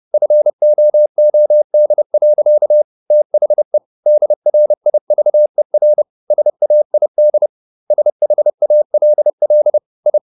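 Morse code, heard as a keyed tone, about 30 wpm, 600 Hz. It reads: FOOD. THE DRIVER SAID SHALL I
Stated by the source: Morse code at 30 wpm, the novel Never by Ken Follett in English, from CW med Gnister.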